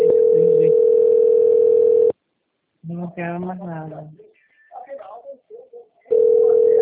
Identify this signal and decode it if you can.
Ringback tone (North America), 440 + 480 Hz, with speech.
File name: out- - - - - .9604